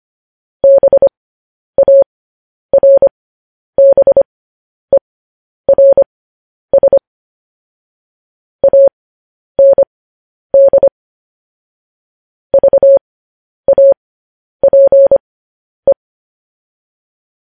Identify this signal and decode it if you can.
Morse code, heard as a keyed tone, about 25 wpm, spaced as for 10 wpm, 550 Hz. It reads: BARBERS AND VAPE
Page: news headlines in morse code at 10 wpm